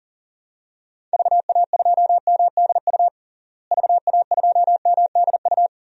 Morse code, heard as a keyed tone, about 37 wpm, 700 Hz. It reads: VA2MDU VA2MDU
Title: DXCC-100 Call Signs - 6 Characters - Repeated 3x - 40wpm